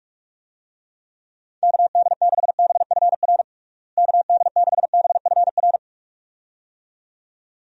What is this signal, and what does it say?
Morse code, heard as a keyed tone, about 46 wpm, 700 Hz. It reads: KD6BFR KD6BFR
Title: USA Call Signs - 6 Characters - Repeated 2x - 45wpm